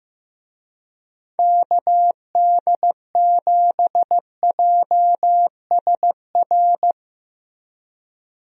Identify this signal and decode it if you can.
Morse code, heard as a keyed tone, about 15 wpm, 700 Hz. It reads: KD7JSR
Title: USA Call Signs - 6 Characters - Repeated 3x - 15wpm